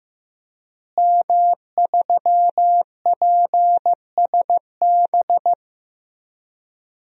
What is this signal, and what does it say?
Morse code, heard as a keyed tone, about 15 wpm, 700 Hz. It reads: M3PSB
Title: DXCC-100 Call Signs - 5 Characters - Repeated 3x - 15wpm